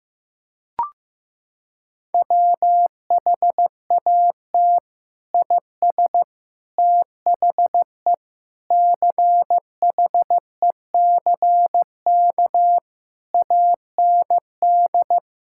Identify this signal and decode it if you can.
Morse code, heard as a keyed tone, about 15 wpm, 700 Hz. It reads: WHAT IS THE CHECK AND
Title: Sentences from Top 1000 Words - Review Part 6 - 15wpm